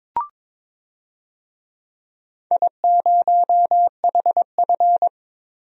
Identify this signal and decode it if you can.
Morse code, heard as a keyed tone, about 22 wpm, 700 Hz. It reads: I0HF
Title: DXCC-100 Call Signs - 4 Characters - 22wpm